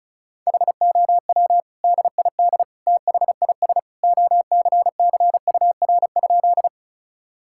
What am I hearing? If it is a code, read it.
Morse code, heard as a keyed tone, about 36 wpm, 700 Hz. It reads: HOW DID THIS OCCUR?
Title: Sentences from Top 1000 Words - Review Part 6 - 35wpm